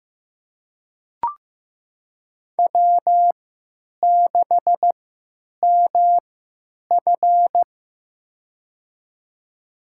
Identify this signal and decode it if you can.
Morse code, heard as a keyed tone, about 15 wpm, spaced as for 9 wpm, 700 Hz. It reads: W6MF
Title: USA Call Signs - 4 Characters - 3x Character Spacing - 15wpm